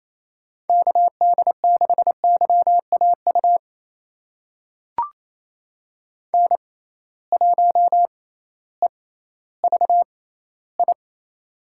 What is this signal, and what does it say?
Morse code, heard as a keyed tone, about 28 wpm, spaced as for 15 wpm, 700 Hz. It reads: KD6YAU N1EVI
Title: USA Call Signs - 6x Character Spacing - 28wpm